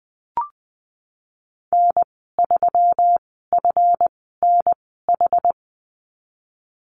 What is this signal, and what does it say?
Morse code, heard as a keyed tone, about 20 wpm, spaced as for 15 wpm, 700 Hz. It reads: N3FNH
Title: 147 Lesson 24g - Letter y - Call Signs 20wpm